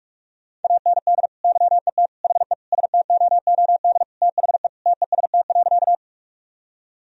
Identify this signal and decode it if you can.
Morse code, heard as a keyed tone, about 41 wpm, 700 Hz. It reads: AND YET HE STOOD THE TEST.
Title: Sentences from Top 1000 Words - Review Part 3 - 45wpm